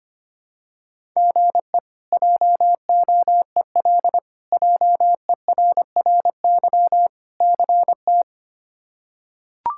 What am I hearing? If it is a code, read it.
Morse code, heard as a keyed tone, about 25 wpm, 700 Hz. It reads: GE JOEL JERRY CT
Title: K1USN SST QSO Practice - 25wpm